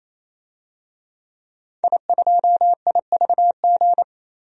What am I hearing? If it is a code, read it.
Morse code, heard as a keyed tone, about 27 wpm, 700 Hz. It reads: I2IVG
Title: DXCC-100 Call Signs - 5 Characters - 28wpm